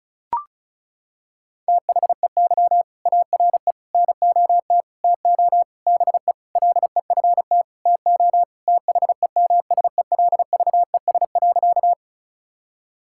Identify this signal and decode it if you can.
Morse code, heard as a keyed tone, about 36 wpm, 700 Hz. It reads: THEY ARE NOT TO BE LEFT TO THEMSELVES.
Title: Sentences from Top 1000 Words - Review Part 5 - 35wpm